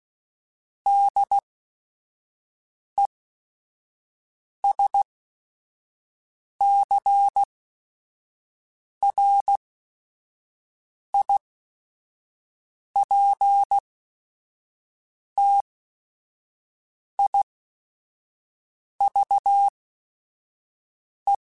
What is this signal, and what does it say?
Morse code, heard as a keyed tone, about 16 wpm, spaced as for 5 wpm, 800 Hz. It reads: DESCRIPTIVE
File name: Technical Subjects / Technical Subjects-mp3s / Reactance @5